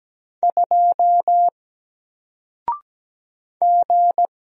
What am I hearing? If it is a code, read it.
Morse code, heard as a keyed tone, about 17 wpm, 700 Hz. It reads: 2 G
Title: Single Character (Letter:Number:Punctuation:Prosign) 17wpm